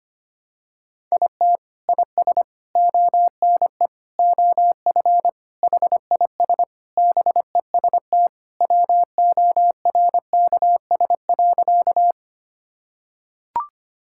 Morse code, heard as a keyed tone, about 25 wpm, 700 Hz. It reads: IT IS ONE OF HIS BEST WORKS.
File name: practice / 25wpm / Sentences with 3-Word Phrases - 25wpm